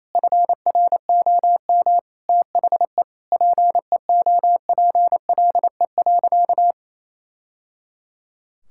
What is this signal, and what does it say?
Morse code, heard as a keyed tone, about 28 wpm, 700 Hz. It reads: FROM THE PEOPLE.
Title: Sentences from Top 1000 Words - Review Part 6 - 28wpm